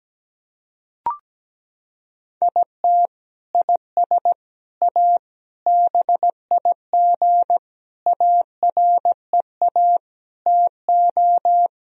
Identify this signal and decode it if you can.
Morse code, heard as a keyed tone, about 17 wpm, 700 Hz. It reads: IT IS A BIG AREA TO
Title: Sentences from Top 1000 Words - Review Part 2 - 17wpm